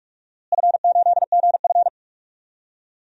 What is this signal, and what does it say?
Morse code, heard as a keyed tone, about 43 wpm, 700 Hz. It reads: F8GF
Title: DXCC-100 Call Signs - 4 Characters - Repeated 2x - 45wpm